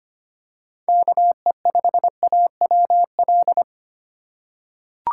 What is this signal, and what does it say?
Morse code, heard as a keyed tone, about 25 wpm, 700 Hz. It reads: KE5AWL
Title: USA Call Signs - 6 Characters - 2x Character Spacing - 25wpm